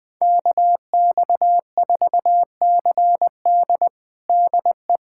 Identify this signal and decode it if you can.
Morse code, heard as a keyed tone, about 20 wpm, 700 Hz. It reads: KX4CD DE
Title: USA Callsign DE Callsign - 5 Characters - 20wpm